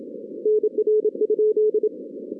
Morse code, heard as a keyed tone, about 26 wpm, 440 Hz. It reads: NR?
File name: 2016 arrl dx cw / 2016-ARRL-DX-CW-K5TR- - 0-41